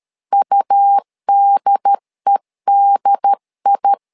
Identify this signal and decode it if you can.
Morse code, heard as a keyed tone, about 13 wpm, 800 Hz. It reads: UDEDI
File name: LEZIONI / 2 UHDN / 2 UHD 12wpm 1540